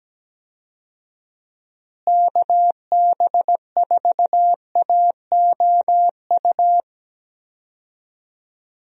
Morse code, heard as a keyed tone, about 17 wpm, 700 Hz. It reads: KB4AOU
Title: USA Call Signs - 6 Characters - Repeated 3x - 17wpm